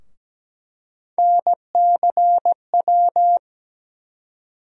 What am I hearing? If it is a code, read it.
Morse code, heard as a keyed tone, about 17 wpm, 700 Hz. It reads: NCW
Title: USA Call Sign Postfixes - 3 Letters 17wpm